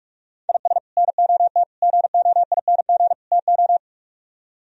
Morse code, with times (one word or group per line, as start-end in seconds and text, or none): 0.5-0.8 IS
1.0-1.6 NOT
1.8-3.1 GOING
3.3-3.8 TO